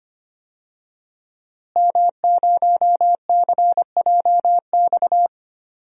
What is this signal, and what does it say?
Morse code, heard as a keyed tone, about 25 wpm, 700 Hz. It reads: M0CJX
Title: DXCC-100 Call Signs - 5 Characters - Repeated 3x - 25wpm